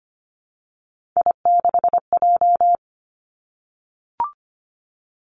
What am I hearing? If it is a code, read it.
Morse code, heard as a keyed tone, about 25 wpm, 700 Hz. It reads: I6J
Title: Three Challenging Letter-Number Combinations 25wpm